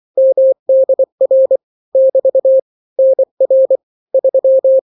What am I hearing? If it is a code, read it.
Morse code, heard as a keyed tone, about 24 wpm, 530 Hz. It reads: MDR <BT> NR 3
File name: daily / 23wpm